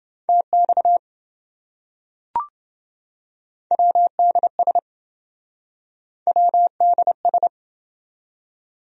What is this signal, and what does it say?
Morse code, heard as a keyed tone, about 28 wpm, 700 Hz. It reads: TX WDS WDS